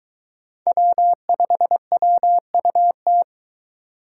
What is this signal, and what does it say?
Morse code, heard as a keyed tone, about 23 wpm, 700 Hz. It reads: W5WUT